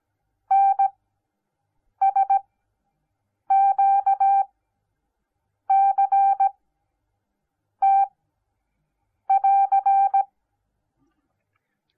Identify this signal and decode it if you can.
Morse code, heard as a keyed tone, about 17 wpm, spaced as for 6 wpm, 780 Hz. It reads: NSQCT<AR>